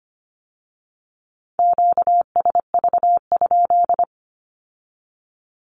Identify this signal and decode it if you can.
Morse code, heard as a keyed tone, about 25 wpm, 700 Hz. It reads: QSV?
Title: Q-Codes with Spoken Meaning - 25wpm